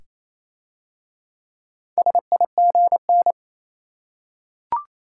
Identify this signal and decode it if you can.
Morse code, heard as a keyed tone, about 27 wpm, 700 Hz. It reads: SIGN
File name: Easy Words - Speed-Racing with Hints - 22wpm